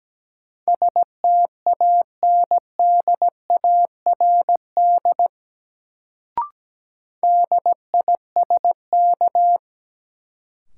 Morse code, heard as a keyed tone, about 17 wpm, 700 Hz. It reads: STANDARD DISK
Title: Sentences from Top 1000 Words - Review Part 6 - New Word Drill - 17wpm